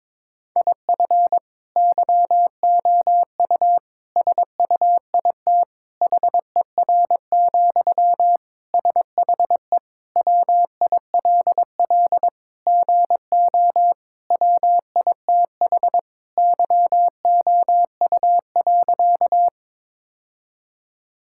Morse code, heard as a keyed tone, about 22 wpm, 700 Hz. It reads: IF YOU SUIT HER, SHE WILL GO WITH YOU.
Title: Sentences from Top 1600 Words 22wpm